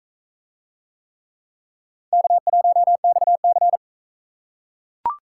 Morse code, heard as a keyed tone, about 44 wpm, 700 Hz. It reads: K1XC